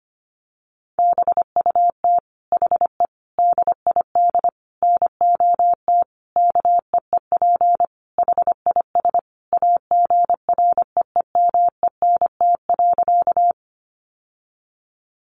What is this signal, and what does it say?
Morse code, heard as a keyed tone, about 25 wpm, 700 Hz. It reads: BUT HE DID NOT KEEP HIS AGREEMENT.
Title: Sentences from Top 1500 Words 25wpm